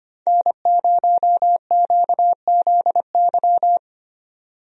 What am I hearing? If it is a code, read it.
Morse code, heard as a keyed tone, about 25 wpm, 700 Hz. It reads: N0QZY
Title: USA Call Signs - 5 Characters - 25wpm